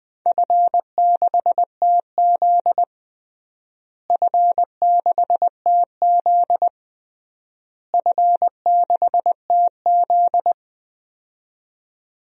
Morse code, heard as a keyed tone, about 21 wpm, 700 Hz. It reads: F6TZ F6TZ F6TZ
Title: DXCC-100 Call Signs - 4 Characters - Repeated 3x - 20wpm